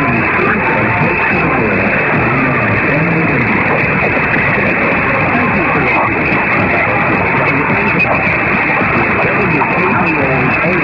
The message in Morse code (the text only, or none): none